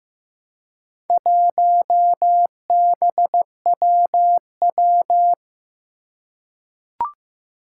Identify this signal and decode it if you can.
Morse code, heard as a keyed tone, about 15 wpm, 700 Hz. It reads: 1BWW
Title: Four Challenging Letter-Number Combinations 15wpm